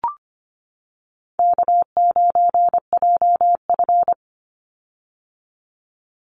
Morse code, heard as a keyed tone, about 25 wpm, 700 Hz. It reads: K9JF
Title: Easy Call Signs - 25wpm